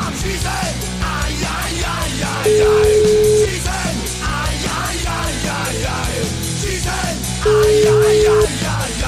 Ringback tone (Europe, 425 Hz), with noise.